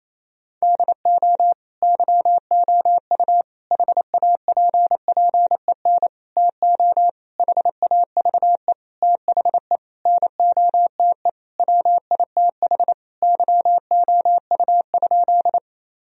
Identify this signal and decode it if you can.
Morse code, heard as a keyed tone, about 28 wpm, 700 Hz. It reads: DO YOU HAPPEN TO HAVE THE NOTE WITH YOU?
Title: Sentences from Top 1000 Words - Review Part 2 - 28wpm